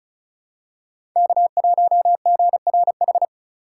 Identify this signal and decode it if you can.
Morse code, heard as a keyed tone, about 36 wpm, 700 Hz. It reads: K1GRH
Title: USA Call Signs - 5 Characters - 6x Character Spacing - 35wpm